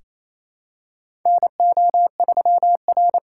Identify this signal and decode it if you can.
Morse code, heard as a keyed tone, about 28 wpm, 700 Hz. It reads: NO3R
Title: USA Callsign DE Callsign - 4 Characters - 28wpm